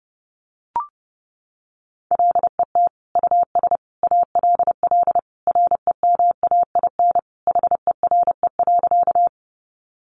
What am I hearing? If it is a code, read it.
Morse code, heard as a keyed tone, about 28 wpm, 700 Hz. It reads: LET US ALL REMAIN HERE.
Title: Sentences from Top 1200 Words 30wpm